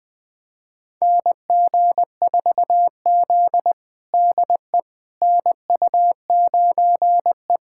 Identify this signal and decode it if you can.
Morse code, heard as a keyed tone, about 20 wpm, 700 Hz. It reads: NG4Z DE NU9E
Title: USA Callsign DE Callsign - 4 Characters - 20wpm